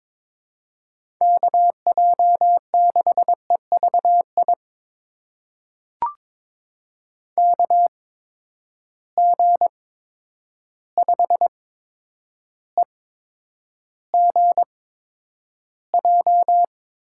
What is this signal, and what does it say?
Morse code, heard as a keyed tone, about 21 wpm, spaced as for 9 wpm, 700 Hz. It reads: KJ6EVI KG5EGJ
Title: USA Call Signs - 6 Characters - 8x Character Spacing - 22wpm